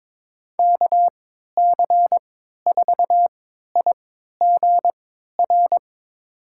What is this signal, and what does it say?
Morse code, heard as a keyed tone, about 22 wpm, spaced as for 12 wpm, 700 Hz. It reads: KC4IGR
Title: USA Call Signs - 3x Character Spacing - 22wpm